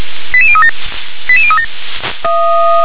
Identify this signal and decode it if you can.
Five-tone selective call (ZVEI), heard as quick runs of five tones, tones of about 70 ms. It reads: 88037 88037
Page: SDR/ZVEI